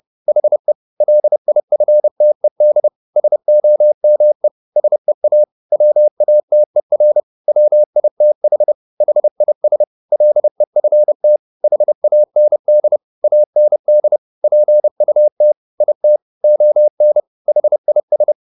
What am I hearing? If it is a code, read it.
Morse code, heard as a keyed tone, about 30 wpm, 600 Hz. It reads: HE LIFTED SOME SEA WATER WITH HIS LEFT HAND AND PUT IT ON HIS